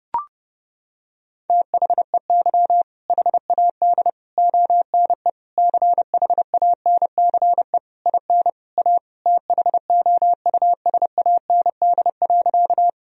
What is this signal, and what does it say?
Morse code, heard as a keyed tone, about 30 wpm, 700 Hz. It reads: THEY HAD ONE CHANCE IN A THOUSAND.
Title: Sentences from Top 1000 Words - Review Part 3 - 30wpm